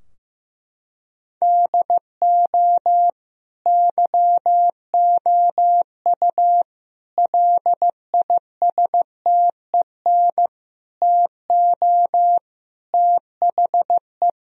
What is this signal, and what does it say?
Morse code, heard as a keyed tone, about 15 wpm, 700 Hz. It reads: DO YOU LISTEN TO THE
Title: Sentences from Top 1000 Words - Review Part 3 - 15wpm